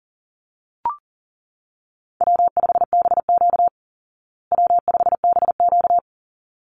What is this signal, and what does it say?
Morse code, heard as a keyed tone, about 40 wpm, 700 Hz. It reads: W5BQ W5BQ
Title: Call Signs Repeated 3x - 40wpm